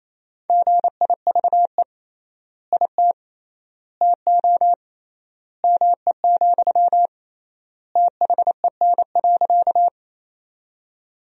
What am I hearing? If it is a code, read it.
Morse code, heard as a keyed tone, about 27 wpm, 700 Hz. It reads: GIVE IT TO ME, THEN.
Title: 3x Word-Spacing - Sentences from Top 200 Words - 28wpm